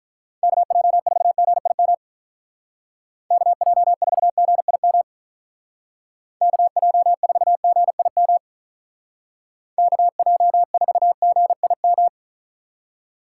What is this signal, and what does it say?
Morse code, heard as a keyed tone, about 47 wpm, 700 Hz. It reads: KJ4GIM KJ4GIM KJ4GIM KJ4GIM